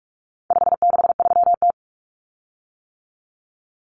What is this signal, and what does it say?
Morse code, heard as a keyed tone, about 44 wpm, 700 Hz. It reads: 563T
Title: Common Cut Numbers - 4-Digit 45wpm